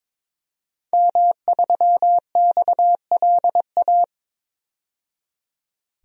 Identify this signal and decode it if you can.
Morse code, heard as a keyed tone, about 21 wpm, 700 Hz. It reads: M3XLA
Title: DXCC-100 Call Signs - 5 Characters - 22wpm